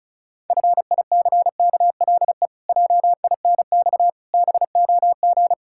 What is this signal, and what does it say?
Morse code, heard as a keyed tone, about 36 wpm, 700 Hz. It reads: FICKLE JINX BOG
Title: Pangram Challenge v2 - 35wpm